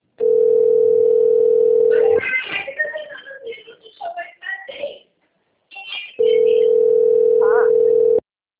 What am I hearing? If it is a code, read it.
Ringback tone (North America), 440 + 480 Hz.